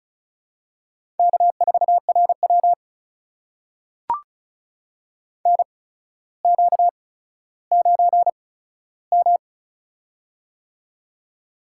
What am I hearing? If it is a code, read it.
Morse code, heard as a keyed tone, about 36 wpm, spaced as for 15 wpm, 700 Hz. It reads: K4RW NQ9M
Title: USA Call Signs - 4 Characters - 8x Character Spacing - 35wpm